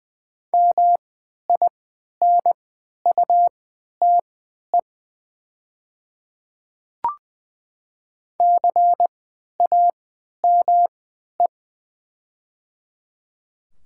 Morse code, heard as a keyed tone, about 20 wpm, spaced as for 11 wpm, 700 Hz. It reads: MINUTE CAME
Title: 3x Character Spacing - Top 500 Words - 20wpm